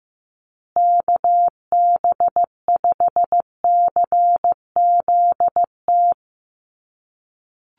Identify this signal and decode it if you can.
Morse code, heard as a keyed tone, about 14 wpm, 700 Hz. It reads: KB5CZT
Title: USA Call Signs Repeated 2x 15wpm